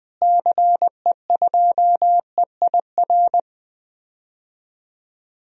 Morse code, heard as a keyed tone, about 20 wpm, 700 Hz. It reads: CE2EIR
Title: DXCC-100 Call Signs - 6 Characters - Repeated 3x - 20wpm